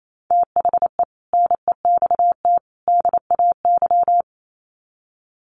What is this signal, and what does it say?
Morse code, heard as a keyed tone, about 28 wpm, 700 Hz. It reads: THE NEXT DAY
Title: Sets of 3 Words 28wpm